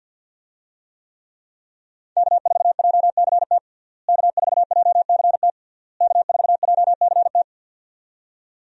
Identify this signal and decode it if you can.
Morse code, heard as a keyed tone, about 52 wpm, 700 Hz. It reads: K4JCT K4JCT K4JCT
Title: USA Call Signs - 5 Characters - Repeated 3x - 50wpm